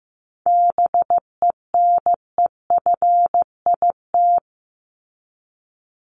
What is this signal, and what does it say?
Morse code, heard as a keyed tone, about 15 wpm, 700 Hz. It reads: BENEFIT
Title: Top 751-1000 Words 15wpm